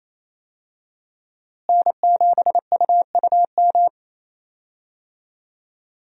Morse code, heard as a keyed tone, about 28 wpm, 700 Hz. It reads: N7UUM
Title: USA Call Signs - 5 Characters - Repeated 3x - 28wpm